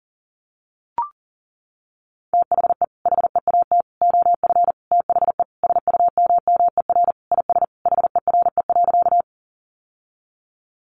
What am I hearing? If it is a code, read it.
Morse code, heard as a keyed tone, about 40 wpm, 700 Hz. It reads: THE HEAT OF THE SUMMER IS HERE.
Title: Sentences from Top 1100 Words 40wpm